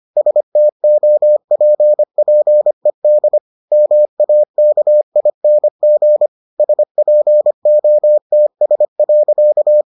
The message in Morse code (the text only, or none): STOPPED MAKING SPOTS.